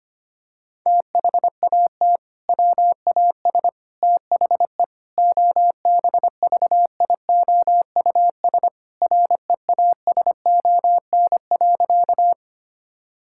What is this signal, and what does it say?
Morse code, heard as a keyed tone, about 25 wpm, 700 Hz. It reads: THAT WAS THE OBVIOUS REASON.
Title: Sentences from Top 1600 Words 25wpm